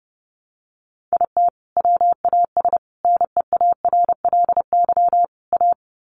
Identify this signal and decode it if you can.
Morse code, heard as a keyed tone, about 30 wpm, 700 Hz. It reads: IT WAS NEARLY A
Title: Sentences from Top 1300 Words 30wpm